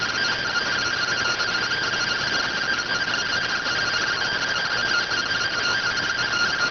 Two-tone teleprinter signal, about 170 Hz shift.